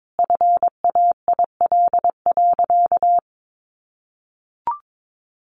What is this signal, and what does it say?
Morse code, heard as a keyed tone, about 22 wpm, 700 Hz. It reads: FAIL.
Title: Sentences from Top 1500 Words 22wpm